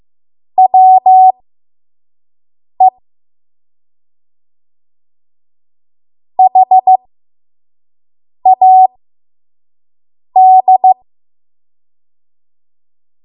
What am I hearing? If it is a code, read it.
Morse code, heard as a keyed tone, about 15 wpm, spaced as for 5 wpm, 750 Hz. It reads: WE HAD
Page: 5 WPM Code Archive